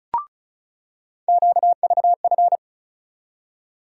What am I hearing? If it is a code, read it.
Morse code, heard as a keyed tone, about 34 wpm, 700 Hz. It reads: QVF
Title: USA Call Sign Postfixes - 3 Letters 35wpm